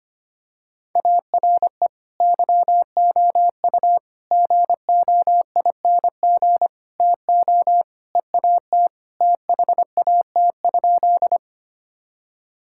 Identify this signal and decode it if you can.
Morse code, heard as a keyed tone, about 25 wpm, 700 Hz. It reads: ARE YOU GOING TO EAT THAT?
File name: practice / 25wpm / Sentences with 4-Word Phrases - 25wpm